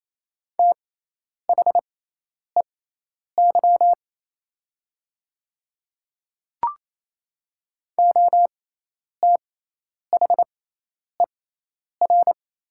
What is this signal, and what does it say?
Morse code, heard as a keyed tone, about 28 wpm, spaced as for 10 wpm, 700 Hz. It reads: THEY OTHER
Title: 6x Character Spacing - Top 100 Words - 28wpm